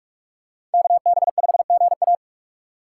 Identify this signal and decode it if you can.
Morse code, heard as a keyed tone, about 46 wpm, 700 Hz. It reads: KB5GA
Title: K1USN SST QSO Practice - 45wpm